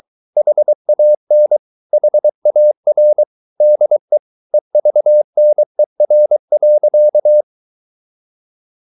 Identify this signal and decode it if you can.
Morse code, heard as a keyed tone, about 23 wpm, 600 Hz. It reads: HAN HAR DE EVNER.
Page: Ferie 23 wpm